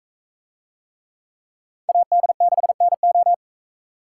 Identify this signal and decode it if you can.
Morse code, heard as a keyed tone, about 40 wpm, 700 Hz. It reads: AD6NO